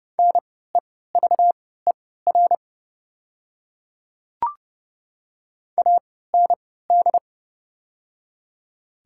Morse code, heard as a keyed tone, about 32 wpm, spaced as for 17 wpm, 690 Hz. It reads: NEVER AND
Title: 3x Character Spacing - Top 500 Words - 30wpm